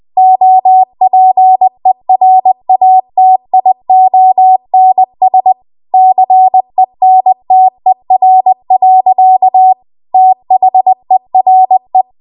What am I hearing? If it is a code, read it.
Morse code, heard as a keyed tone, about 20 wpm, 750 Hz. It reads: OPERATIONS CENTER. THERE
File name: Morse / 20WPM